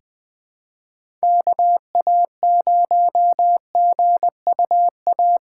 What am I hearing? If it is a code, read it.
Morse code, heard as a keyed tone, about 20 wpm, 700 Hz. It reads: KA0GUA
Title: USA Call Signs - 6 Characters - 2x Character Spacing - 20wpm